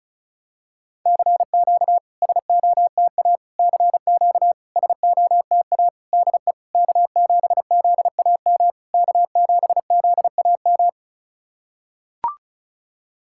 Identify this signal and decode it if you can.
Morse code, heard as a keyed tone, about 36 wpm, 700 Hz. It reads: CQ SOTA CQ SOTA DE K7ZAM K7ZAM
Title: SOTA QSO Practice 35wpm